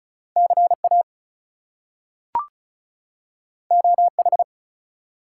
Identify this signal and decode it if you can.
Morse code, heard as a keyed tone, about 36 wpm, 700 Hz. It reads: CA OH
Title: US States and Canadian Provinces 35wpm